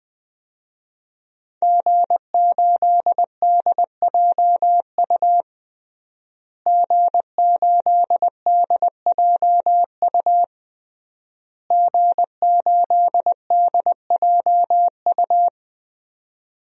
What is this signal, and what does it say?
Morse code, heard as a keyed tone, about 20 wpm, 700 Hz. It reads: G8DJU G8DJU G8DJU
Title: DXCC-100 Call Signs - 5 Characters - Repeated 3x - 20wpm